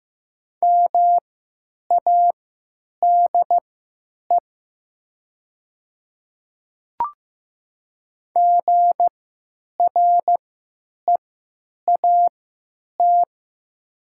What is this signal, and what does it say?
Morse code, heard as a keyed tone, about 15 wpm, spaced as for 9 wpm, 700 Hz. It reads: MADE GREAT
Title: 3x Character Spacing - Top 200 Words - 15wpm